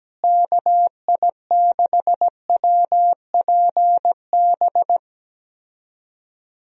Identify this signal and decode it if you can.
Morse code, heard as a keyed tone, about 17 wpm, 700 Hz. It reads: KI6WPB
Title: USA Call Signs - 6 Characters - Repeated 3x - 17wpm